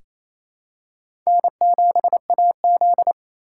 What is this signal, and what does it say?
Morse code, heard as a keyed tone, about 28 wpm, 700 Hz. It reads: N7AZ
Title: K1USN SST QSO Practice - 28wpm